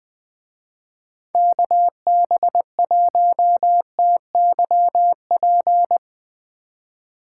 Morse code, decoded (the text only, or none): KB1TYP